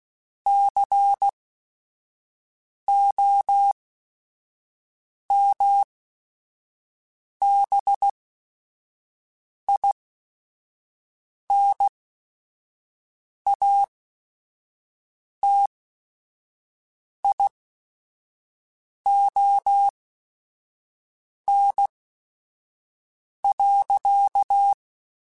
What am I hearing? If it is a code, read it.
Morse code, heard as a keyed tone, about 16 wpm, spaced as for 5 wpm, 800 Hz. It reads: COMBINATION.
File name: Technical Subjects / Technical Subjects-mp3s / Sw Pwr Sup @5